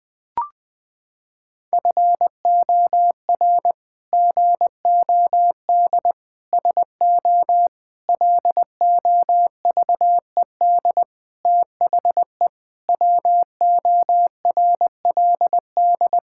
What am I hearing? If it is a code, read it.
Morse code, heard as a keyed tone, about 20 wpm, 700 Hz. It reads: FOR GOD SO LOVED THE WORLD
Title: Sentences with 8-Word Phrases - 20wpm